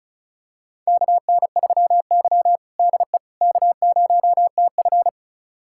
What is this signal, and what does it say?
Morse code, heard as a keyed tone, about 34 wpm, 700 Hz. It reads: KN3Y DE K0TF
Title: USA Callsign DE Callsign - 4 Characters - 35wpm